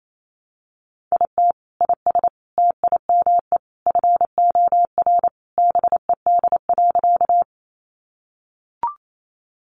Morse code, heard as a keyed tone, about 28 wpm, 700 Hz. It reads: IT IS TIME FOR BED.
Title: Sentences from Top 500 Words 28wpm